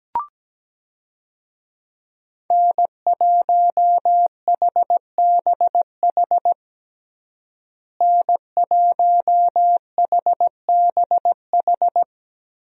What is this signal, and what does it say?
Morse code, heard as a keyed tone, about 17 wpm, 700 Hz. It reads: N1HBH N1HBH
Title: USA Call Signs - 5 Characters - Repeated 2x - 17wpm